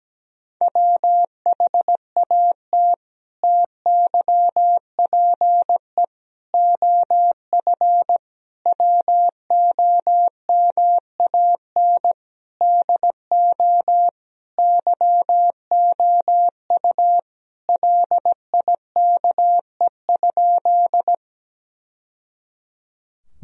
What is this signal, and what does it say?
Morse code, heard as a keyed tone, about 17 wpm, 700 Hz. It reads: WHAT TYPE OF WOMAN DO YOU LIKE?
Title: Sentences from Top 1000 Words - Review Part 3 - 17wpm